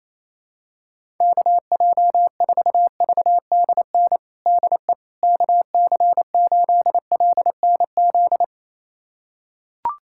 Morse code, heard as a keyed tone, about 28 wpm, 700 Hz. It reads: KJ4VDN DE KC8LNZ
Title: USA Callsign DE Callsign - 6 Characters - 28wpm